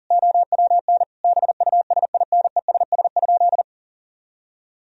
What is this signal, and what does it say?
Morse code, heard as a keyed tone, about 39 wpm, 700 Hz. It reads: OWN BUSINESS?
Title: Sentences from Top 1000 Words - Review Part 4 - 40wpm